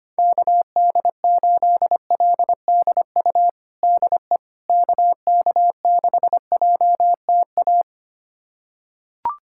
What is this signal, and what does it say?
Morse code, heard as a keyed tone, about 25 wpm, 700 Hz. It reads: KD8LDU DE KK6JTA
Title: USA Callsign DE Callsign - 6 Characters - 25wpm